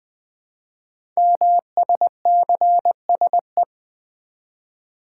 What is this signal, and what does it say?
Morse code, heard as a keyed tone, about 20 wpm, 700 Hz. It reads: MSCSE
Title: Code Groups - 5 Letter:Numbers - 20wpm